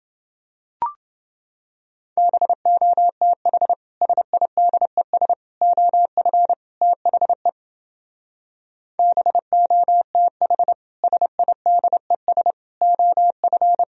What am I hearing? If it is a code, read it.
Morse code, heard as a keyed tone, about 28 wpm, 700 Hz. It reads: BOTH SIDES OF THE BOTH SIDES OF